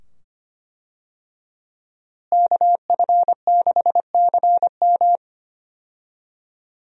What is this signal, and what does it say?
Morse code, heard as a keyed tone, about 25 wpm, 700 Hz. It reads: KF6CM
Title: USA Call Signs - 5 Characters - Repeated 2x - 25wpm